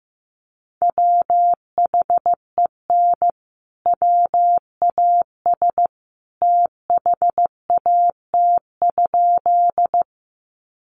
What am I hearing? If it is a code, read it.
Morse code, heard as a keyed tone, about 15 wpm, 700 Hz. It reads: WHEN WAS THAT?
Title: Sentences from Top 100 Words 15wpm